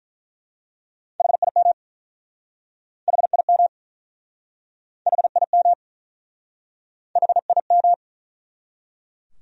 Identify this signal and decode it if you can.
Morse code, heard as a keyed tone, about 47 wpm, 700 Hz. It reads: HIM HIM HIM HIM